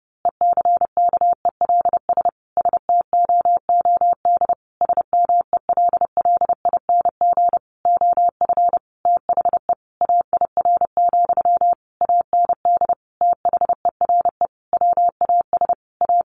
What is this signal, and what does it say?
Morse code, heard as a keyed tone, about 30 wpm, 700 Hz. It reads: ECKELS STOOD SMELLING OF THE AIR, AND THERE WAS A
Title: A Sound Of Thunder 30wpm